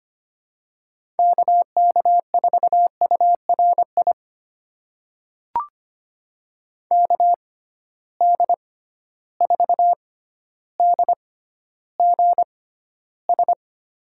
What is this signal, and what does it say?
Morse code, heard as a keyed tone, about 25 wpm, spaced as for 13 wpm, 700 Hz. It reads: KK4URI KD4DGS
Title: USA Call Signs - 6 Characters - 6x Character Spacing - 25wpm